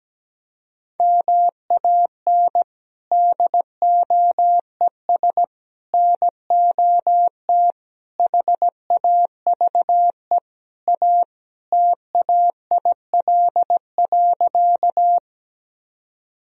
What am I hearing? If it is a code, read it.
Morse code, heard as a keyed tone, about 17 wpm, 700 Hz. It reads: MAN DOES NOT HAVE A TAIL.
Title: Sentences from Top 1000 Words - Review Part 3 - 17wpm